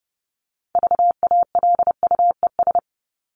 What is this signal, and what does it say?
Morse code, heard as a keyed tone, about 30 wpm, 700 Hz. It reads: VALUES
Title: ES:S Suffix - Top 100 Words with Base Words 30wpm